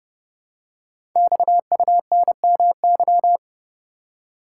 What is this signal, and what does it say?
Morse code, heard as a keyed tone, about 30 wpm, 700 Hz. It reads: XUNMY
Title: Code Groups - 5 Letters - 30wpm